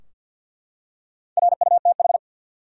Morse code, heard as a keyed tone, about 50 wpm, 700 Hz. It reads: RUTH
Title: USA Common Names - Max 4 Letters - 50wpm